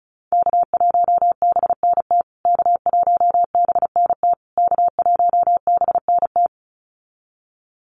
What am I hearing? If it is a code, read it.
Morse code, heard as a keyed tone, about 34 wpm, 700 Hz. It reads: K1BNT K1BNT K1BNT
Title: USA Call Signs Repeated 3x 35wpm